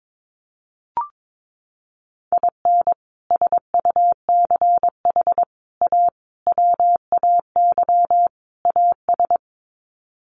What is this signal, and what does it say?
Morse code, heard as a keyed tone, about 22 wpm, 700 Hz. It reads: IN SUCH A WAY AS
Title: Sets of 5 Words Encore 22wpm